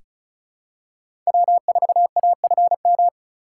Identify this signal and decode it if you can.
Morse code, heard as a keyed tone, about 34 wpm, 700 Hz. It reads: W4AFM
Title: USA Call Signs - 5 Characters - 4x Character Spacing - 35wpm